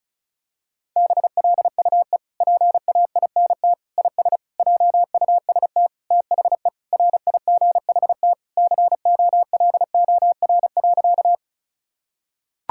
Morse code, heard as a keyed tone, about 36 wpm, 700 Hz. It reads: BLUE PAINT IS JUST THE RIGHT COLOR.
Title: Sentences from Top 1000 Words - Review Part 6 - 35wpm